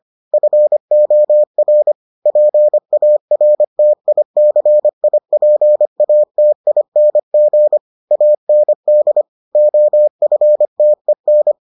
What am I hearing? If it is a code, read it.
Morse code, heard as a keyed tone, about 25 wpm, 600 Hz. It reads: FOR PARTICIPATING AND OFTEN